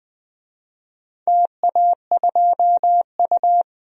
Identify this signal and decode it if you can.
Morse code, heard as a keyed tone, about 20 wpm, 700 Hz. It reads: TA2U
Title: DXCC-100 Call Signs - 4 Characters - 20wpm